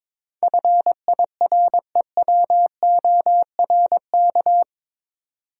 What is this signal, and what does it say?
Morse code, heard as a keyed tone, about 22 wpm, 700 Hz. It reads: FIREWORK
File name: practice / 22wpm / Eight Letter Words - 22wpm